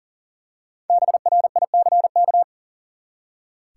Morse code, heard as a keyed tone, about 40 wpm, 700 Hz. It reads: BRICK
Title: Five Letter Words - 40wpm